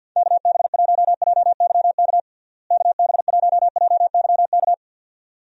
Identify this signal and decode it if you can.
Morse code, heard as a keyed tone, about 52 wpm, 700 Hz. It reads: KB1JYK KB1JYK
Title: USA Call Signs - 6 Characters - Repeated 2x - 50wpm